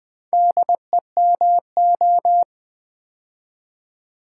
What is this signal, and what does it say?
Morse code, heard as a keyed tone, about 20 wpm, 700 Hz. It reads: DEMO
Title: Build A Word - 4 Letters At A Time - 11-Letter Words - 20wpm